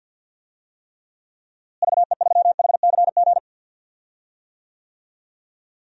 Morse code, heard as a keyed tone, about 53 wpm, 700 Hz. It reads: VE3HCG
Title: DXCC-100 Call Signs - 6 Characters - Repeated 3x - 50wpm